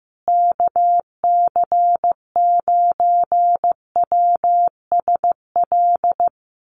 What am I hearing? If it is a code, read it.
Morse code, heard as a keyed tone, about 15 wpm, 700 Hz. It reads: KC9WSL
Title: USA Call Signs Repeated 2x 15wpm